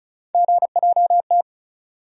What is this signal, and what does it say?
Morse code, heard as a keyed tone, about 34 wpm, 700 Hz. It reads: GJT
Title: USA Call Sign Postfixes - 3 Letters 35wpm